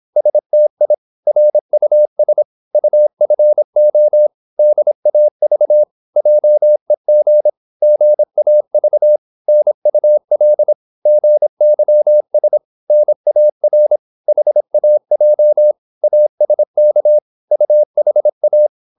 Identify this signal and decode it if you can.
Morse code, heard as a keyed tone, about 26 wpm, 600 Hz. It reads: STI RUS UFO DAV JEG GAV NUL GYS NAR HAJ ASK UHA
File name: Korte ord DK 26wpm